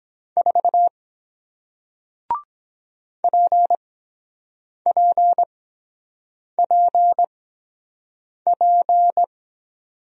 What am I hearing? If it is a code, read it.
Morse code, heard as a keyed tone, about 23 wpm, spaced as for 7 wpm, 700 Hz. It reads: 4 PPPP